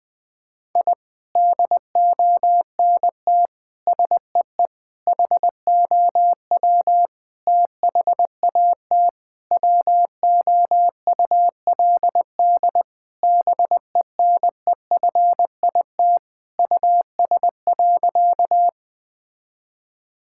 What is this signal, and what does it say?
Morse code, heard as a keyed tone, about 20 wpm, 700 Hz. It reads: I DONT SEE HOW THAT WOULD BENEFIT US.
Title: Sentences from Top 1000 Words - Review Part 6 - 20wpm